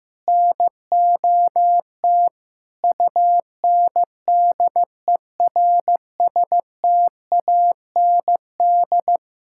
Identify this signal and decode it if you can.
Morse code, heard as a keyed tone, about 15 wpm, 700 Hz. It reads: NOT UNDERSTAND
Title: Sentences from Top 1000 Words - Review Part 5 - 15wpm